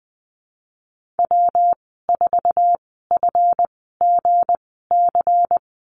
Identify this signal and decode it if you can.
Morse code, heard as a keyed tone, about 20 wpm, spaced as for 15 wpm, 700 Hz. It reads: W4FGC
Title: 179 Lesson 29g - Slash - Call Signs 20wpm